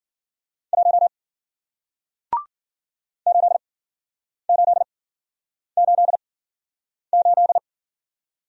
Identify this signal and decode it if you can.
Morse code, heard as a keyed tone, about 51 wpm, spaced as for 9 wpm, 700 Hz. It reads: 1 8888